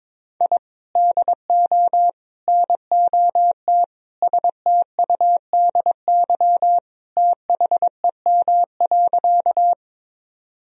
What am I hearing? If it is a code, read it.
Morse code, heard as a keyed tone, about 22 wpm, 700 Hz. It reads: I DO NOT STUDY THEM.
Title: Sentences from Top 1000 Words - Review Part 1 - 22wpm